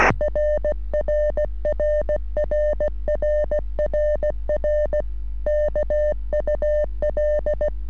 Morse code, heard as a keyed tone, about 16 wpm, 600 Hz. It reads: RRRRRRR KUL
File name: sm7lcb / sm7gep 3cm rs